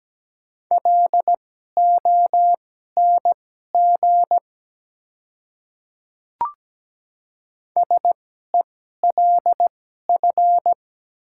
Morse code, heard as a keyed tone, about 17 wpm, spaced as for 12 wpm, 700 Hz. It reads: LONG SELF
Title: 2x Character Spacing - Top 200 Words - 17wpm